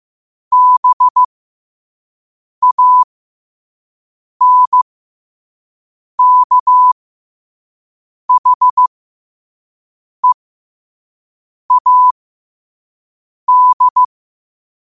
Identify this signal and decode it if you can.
Morse code, heard as a keyed tone, about 15 wpm, spaced as for 5 wpm, 1000 Hz. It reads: BANKHEAD
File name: 5wpm